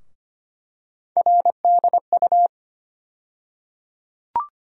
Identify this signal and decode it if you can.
Morse code, heard as a keyed tone, about 25 wpm, 700 Hz. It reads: RDU
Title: USA Call Sign Postfixes - 3 Letters 25wpm